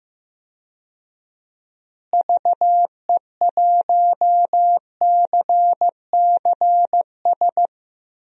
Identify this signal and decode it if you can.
Morse code, heard as a keyed tone, about 15 wpm, 700 Hz. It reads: VE1CCS
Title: DXCC-100 Call Signs - 6 Characters - Repeated 2x - 15wpm